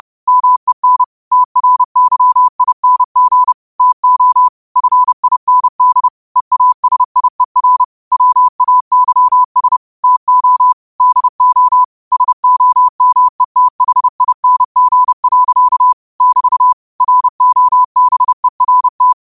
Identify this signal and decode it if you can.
Morse code, heard as a keyed tone, about 30 wpm, 1000 Hz. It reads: MEN TRYING TO FIND EASIER WAYS TO DO SOMETHING. <BT> ROBERT
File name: qotdfiles / 30wpm